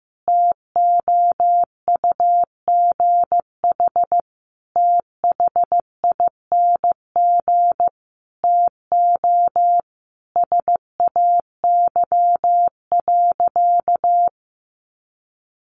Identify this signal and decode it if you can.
Morse code, heard as a keyed tone, about 15 wpm, 700 Hz. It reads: TOUGH THING TO SAY.
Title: Sentences from Top 1200 Words 15wpm